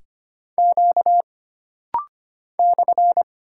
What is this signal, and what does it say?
Morse code, heard as a keyed tone, about 25 wpm, 700 Hz. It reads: Q /
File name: Single Character (Letter:Number:Punctuation:Prosign) - Rapid-Fire - 25wpm